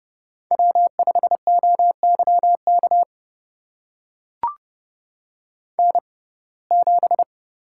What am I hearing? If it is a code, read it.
Morse code, heard as a keyed tone, about 31 wpm, spaced as for 22 wpm, 700 Hz. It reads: W5OYK N7
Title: USA Call Signs - 5 Characters - 6x Character Spacing - 30wpm